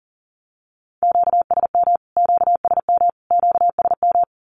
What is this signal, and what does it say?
Morse code, heard as a keyed tone, about 40 wpm, 700 Hz. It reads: QSM QSM QSM
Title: Q-Codes Repeated 3x 40wpm